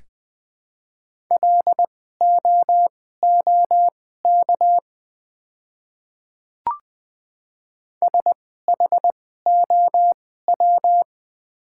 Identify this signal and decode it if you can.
Morse code, heard as a keyed tone, about 20 wpm, spaced as for 15 wpm, 690 Hz. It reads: LOOK SHOW
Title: 2x Character Spacing - Top 500 Words - 20wpm